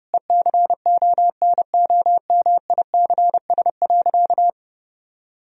Morse code, heard as a keyed tone, about 30 wpm, 700 Hz. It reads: ECONOMICS.
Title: Sentences from Top 1000 Words - Review Part 3 - 30wpm